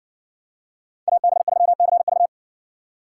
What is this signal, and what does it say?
Morse code, heard as a keyed tone, about 58 wpm, 700 Hz. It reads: AB3CV